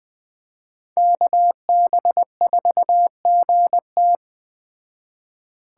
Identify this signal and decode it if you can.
Morse code, heard as a keyed tone, about 20 wpm, 700 Hz. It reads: KB4GT
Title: Speed Racing - USA 2x2 Call Signs 17wpm